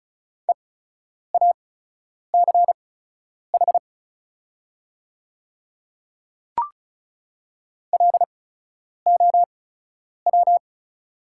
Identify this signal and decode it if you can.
Morse code, heard as a keyed tone, about 36 wpm, spaced as for 10 wpm, 700 Hz. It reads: EACH LOW
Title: 8x Character Spacing - Top 200 Words - 35wpm